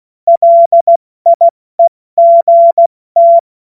Morse code, heard as a keyed tone, about 16 wpm, 680 Hz. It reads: LIEGT
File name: Morsekurs / Witze / Witz 21. Dezember 2021, 70 F80